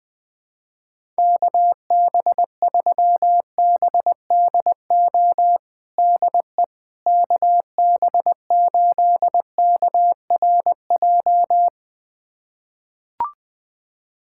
Morse code, heard as a keyed tone, about 20 wpm, 700 Hz. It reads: KB3BDO DE KB8KRJ
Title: USA Callsign DE Callsign 20wpm